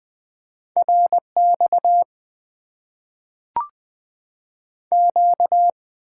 Morse code, heard as a keyed tone, about 20 wpm, 700 Hz. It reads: RX Q
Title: Top 90 QSO Elements - 8x Character Spacing - 20wpm